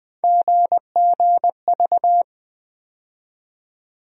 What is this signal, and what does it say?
Morse code, heard as a keyed tone, about 21 wpm, 700 Hz. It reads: GGV
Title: USA Call Sign Postfixes - 3 Letters 20wpm